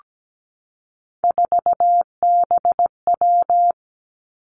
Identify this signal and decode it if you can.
Morse code, heard as a keyed tone, about 17 wpm, 700 Hz. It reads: 4BW
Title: Three Challenging Letter-Number Combinations 17wpm